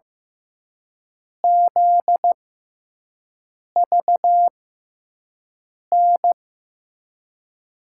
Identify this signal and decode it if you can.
Morse code, heard as a keyed tone, about 15 wpm, 700 Hz. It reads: Z V N
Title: USA Call Signs - 6x Character Spacing - 15wpm